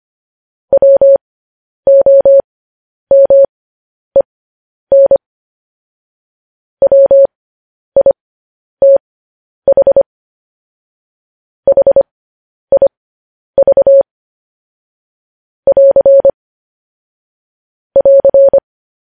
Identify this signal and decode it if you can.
Morse code, heard as a keyed tone, about 25 wpm, spaced as for 10 wpm, 550 Hz. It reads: WOMEN WITH HIV <AR> <AR>